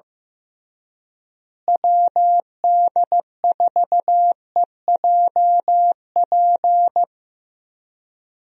Morse code, heard as a keyed tone, about 16 wpm, 700 Hz. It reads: WD4EJP
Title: USA Call Signs - 6 Characters - Repeated 3x - 15wpm